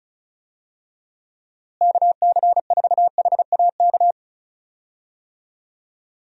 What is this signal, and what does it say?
Morse code, heard as a keyed tone, about 36 wpm, 700 Hz. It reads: KC4HAK
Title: USA Call Signs - 6 Characters - Repeated 2x - 35wpm